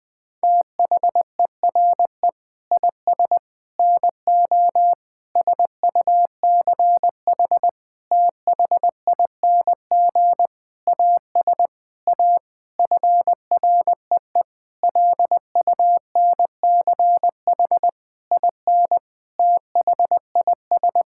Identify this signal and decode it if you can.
Morse code, heard as a keyed tone, about 20 wpm, 700 Hz. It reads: THERE IS NO SUCH THING AS A FREE LUNCH IN THIS
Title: Sentences with 9-Word Phrases - Spoken Beginning - 20wpm